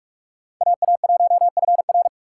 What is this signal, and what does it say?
Morse code, heard as a keyed tone, about 46 wpm, 700 Hz. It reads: AA1FR
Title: USA Call Signs - 5 Characters - 3x Character Spacing - 45wpm